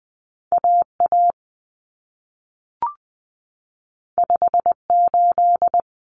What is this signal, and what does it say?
Morse code, heard as a keyed tone, about 20 wpm, 700 Hz. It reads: AA 58
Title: Common Cut Numbers - 2-Digit 20wpm